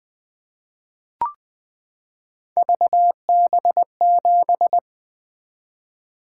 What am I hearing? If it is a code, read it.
Morse code, heard as a keyed tone, about 20 wpm, 700 Hz. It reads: VB7
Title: Three Letter-Number Combinations - 20wpm